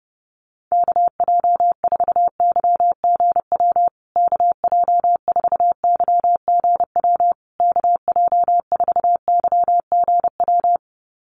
Morse code, heard as a keyed tone, about 30 wpm, 700 Hz. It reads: KJ4YGW KJ4YGW KJ4YGW
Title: USA Call Signs Repeated 3x 30wpm